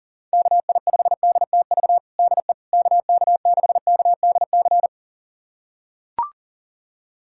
Morse code, heard as a keyed tone, about 40 wpm, 700 Hz. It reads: KI5DTV DE KK6KDC
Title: USA Callsign DE Callsign - 6 Characters - 40wpm